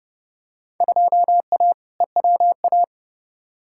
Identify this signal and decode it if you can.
Morse code, heard as a keyed tone, about 30 wpm, 700 Hz. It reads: 2A EWA
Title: Field Day Exchanges - 30wpm